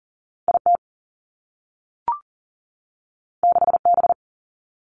Morse code, heard as a keyed tone, about 41 wpm, 700 Hz. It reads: IT 6B
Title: Two Challenging Letter-Number Combinations 40wpm